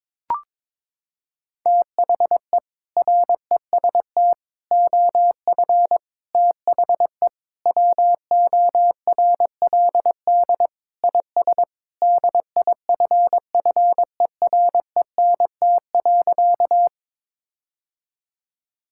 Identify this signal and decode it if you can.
Morse code, heard as a keyed tone, about 22 wpm, 700 Hz. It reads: THE REST OF THE WORLD IS DIFFERENT.
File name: practice / 22wpm / Sentences with 4-Word Phrases - 22wpm